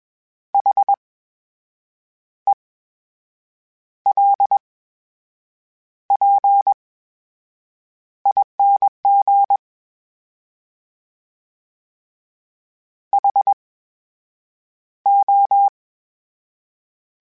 Morse code, heard as a keyed tone, about 21 wpm, spaced as for 7 wpm, 800 Hz. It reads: HELPING HO
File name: Special INGs-mp3s / Special INGs-mp3s / VerySlowToFaster Ing @20